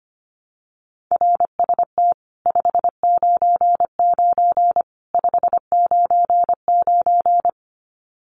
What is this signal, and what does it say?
Morse code, heard as a keyed tone, about 25 wpm, 700 Hz. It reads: RST 599 599
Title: Common QSO Phrases 25wpm